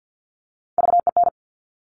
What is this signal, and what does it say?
Morse code, heard as a keyed tone, about 50 wpm, 700 Hz. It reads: VEN
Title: Top Three Letter Combinations in Words 50wpm